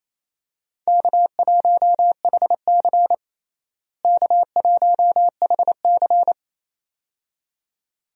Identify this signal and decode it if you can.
Morse code, heard as a keyed tone, about 28 wpm, 700 Hz. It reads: K1HC K1HC
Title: USA Call Signs - 4 Characters - Repeated 2x - 28wpm